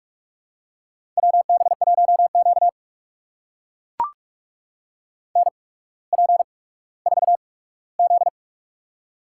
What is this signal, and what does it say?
Morse code, heard as a keyed tone, about 44 wpm, spaced as for 19 wpm, 690 Hz. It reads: WB1Q NP4Z
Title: USA Call Signs - 4 Characters - 8x Character Spacing - 45wpm